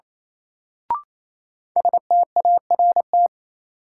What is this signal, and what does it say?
Morse code, heard as a keyed tone, about 28 wpm, 700 Hz. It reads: START
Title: Sentences from Top 300 Words - New Word Drill - 28wpm